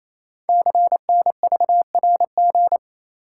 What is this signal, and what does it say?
Morse code, heard as a keyed tone, about 27 wpm, 700 Hz. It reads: CNVRG
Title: Code Groups - 5 Letters - 28wpm